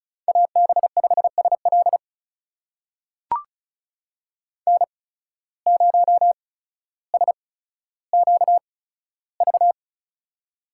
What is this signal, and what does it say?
Morse code, heard as a keyed tone, about 36 wpm, spaced as for 15 wpm, 700 Hz. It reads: AB5SL N0SQV
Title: USA Call Signs - 5 Characters - 8x Character Spacing - 35wpm